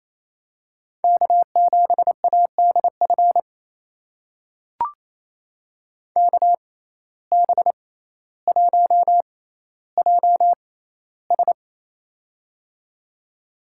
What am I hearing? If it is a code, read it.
Morse code, heard as a keyed tone, about 28 wpm, spaced as for 14 wpm, 700 Hz. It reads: K7ADF KB1JS
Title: USA Call Signs - 5 Characters - 6x Character Spacing - 28wpm